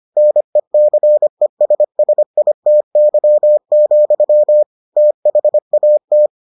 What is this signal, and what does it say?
Morse code, heard as a keyed tone, about 25 wpm, 600 Hz. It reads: NECESSITY, THAT